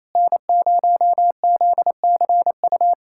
Morse code, heard as a keyed tone, about 27 wpm, 700 Hz. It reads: N0ZCU
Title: USA Callsign DE Callsign 28wpm